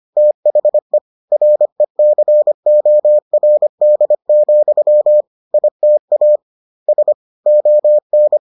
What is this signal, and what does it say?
Morse code, heard as a keyed tone, about 25 wpm, 600 Hz. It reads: THE RECORD, ITA S ON